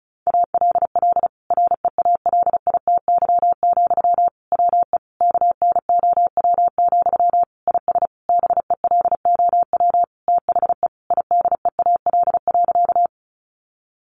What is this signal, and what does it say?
Morse code, heard as a keyed tone, about 34 wpm, 700 Hz. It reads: ALL REALITY, WE KNOW, IS BELOW THE IDEAL.
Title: Sentences from Top 1400 Words 35wpm